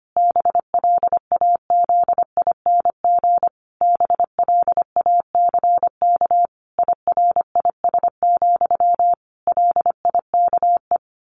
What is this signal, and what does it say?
Morse code, heard as a keyed tone, about 25 wpm, 700 Hz. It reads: BLAZING BLACK IRIS, LIKE
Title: A Sound Of Thunder 25wpm